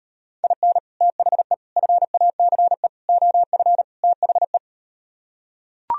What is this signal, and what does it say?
Morse code, heard as a keyed tone, about 38 wpm, 700 Hz. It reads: IN THE FACE OF THE